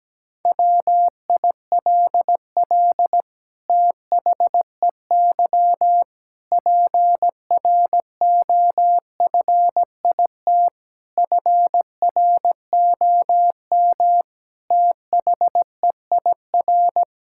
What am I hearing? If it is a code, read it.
Morse code, heard as a keyed tone, about 17 wpm, 700 Hz. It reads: WILL THEY PROFIT FROM THEIR
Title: Sentences from Top 1000 Words - Review Part 6 - 17wpm